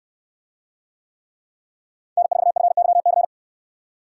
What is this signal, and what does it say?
Morse code, heard as a keyed tone, about 68 wpm, 700 Hz. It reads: N4FYC